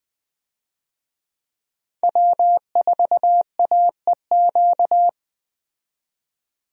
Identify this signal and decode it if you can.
Morse code, heard as a keyed tone, about 20 wpm, 700 Hz. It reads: W4AEQ
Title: USA Call Signs - 5 Characters - 20wpm